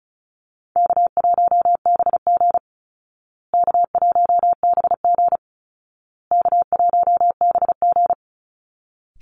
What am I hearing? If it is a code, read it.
Morse code, heard as a keyed tone, about 34 wpm, 700 Hz. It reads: K1BG K1BG K1BG
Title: Easy Call Signs Repeated 3x - 35wpm